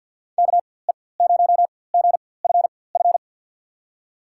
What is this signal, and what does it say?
Morse code, heard as a keyed tone, about 52 wpm, spaced as for 24 wpm, 700 Hz. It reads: KE0GFF
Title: USA Call Signs - 6 Characters - 4x Character Spacing - 50wpm